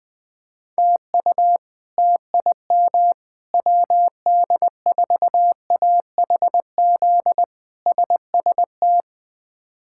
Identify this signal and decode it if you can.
Morse code, heard as a keyed tone, about 20 wpm, 700 Hz. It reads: TU TIM WD4AHZ SST
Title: K1USN SST QSO Practice - 20wpm